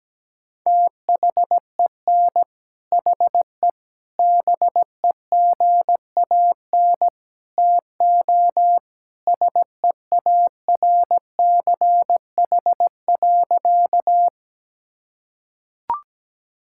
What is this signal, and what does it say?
Morse code, heard as a keyed tone, about 17 wpm, 700 Hz. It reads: THEN HE BEGAN TO SEARCH.
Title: Sentences from Top 1000 Words - Review Part 5 - 17wpm